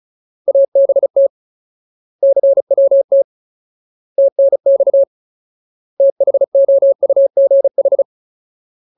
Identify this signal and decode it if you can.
Morse code, heard as a keyed tone, about 34 wpm, 550 Hz. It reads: ABT CWT TNX THOUGH
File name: Ultimate Word List / 35WPM Ultimate